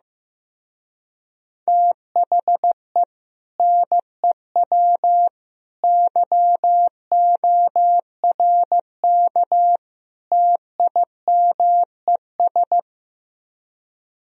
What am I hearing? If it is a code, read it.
Morse code, heard as a keyed tone, about 15 wpm, 700 Hz. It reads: THE NEW YORK TIMES
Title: Speed-Racing - Sets of 4 Words 15wpm